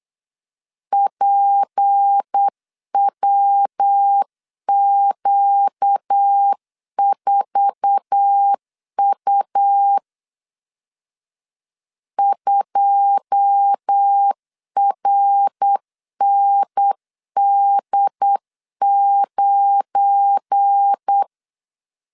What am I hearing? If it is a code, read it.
Morse code, heard as a keyed tone, about 8 wpm, 800 Hz. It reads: PWQ4U 2RND9
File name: CW / DET / 1 9 1500C S A 8wpm 5637